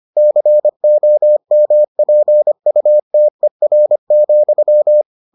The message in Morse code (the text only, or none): COMPUTER,